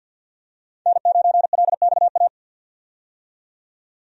Morse code, heard as a keyed tone, about 50 wpm, 700 Hz. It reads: N9LXA
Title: USA Call Signs - 5 Characters - 50wpm